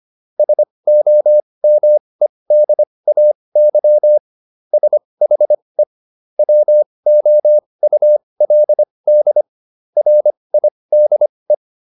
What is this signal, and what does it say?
Morse code, heard as a keyed tone, about 25 wpm, spaced as for 20 wpm, 600 Hz. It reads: SOMEDAY SHE WOULD RIDE
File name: SS123 20